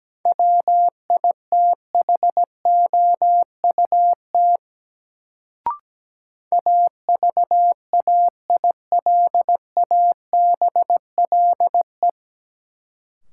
Morse code, heard as a keyed tone, about 17 wpm, 700 Hz. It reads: WITHOUT AVAILABLE
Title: Sentences from Top 600 Words - New Word Drill - 17wpm